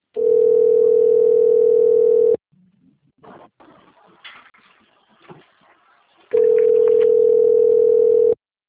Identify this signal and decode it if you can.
Ringback tone (North America), 440 + 480 Hz.
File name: rg-1600-unknown- - - .34470